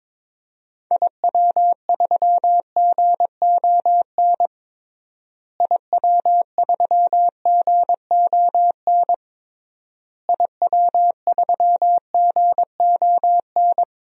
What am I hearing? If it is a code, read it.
Morse code, heard as a keyed tone, about 22 wpm, 700 Hz. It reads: IW3GON IW3GON IW3GON
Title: DXCC-100 Call Signs - 6 Characters - Repeated 3x - 22wpm